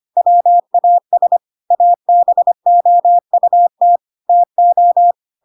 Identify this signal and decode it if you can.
Morse code, heard as a keyed tone, about 25 wpm, 700 Hz. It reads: WAS ABOUT TO
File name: Pinocchio / Pinocchio ch2 25 25 ik1hge